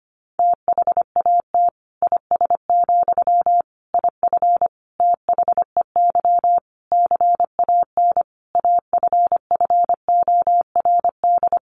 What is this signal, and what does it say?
Morse code, heard as a keyed tone, about 25 wpm, 700 Hz. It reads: THAT IS, IF THEY CAN AFFORD
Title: Sentences from Top 1100 Words 25wpm